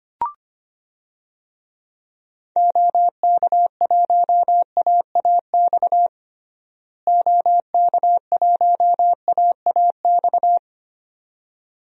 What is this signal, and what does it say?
Morse code, heard as a keyed tone, about 25 wpm, 700 Hz. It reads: OK1AAX OK1AAX
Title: DXCC-100 Call Signs - 6 Characters - Repeated 2x - 25wpm